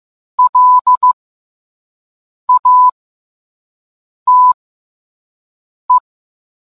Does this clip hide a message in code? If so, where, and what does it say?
Yes: Morse code from 0.4 to 6.0 s, heard as a keyed tone, about 15 wpm, spaced as for 5 wpm, 1000 Hz. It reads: LATE